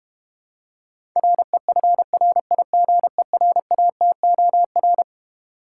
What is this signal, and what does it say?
Morse code, heard as a keyed tone, about 31 wpm, 700 Hz. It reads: REFRIGERATOR
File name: Easy Words - Speed-Racing with Hints - 25wpm